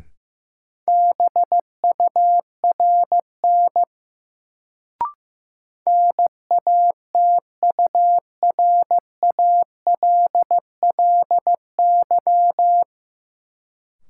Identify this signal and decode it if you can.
Morse code, heard as a keyed tone, about 15 wpm, 700 Hz. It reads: BURN NATURALLY
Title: Sentences from Top 1300 Words - New Word Drill - 15wpm